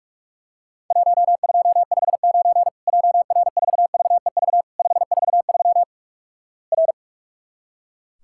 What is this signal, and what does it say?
Morse code, heard as a keyed tone, about 46 wpm, 700 Hz. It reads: 1259 JR4VEV 543 R
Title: Worked All Europe DX Contest - QTC Exchange - 45wpm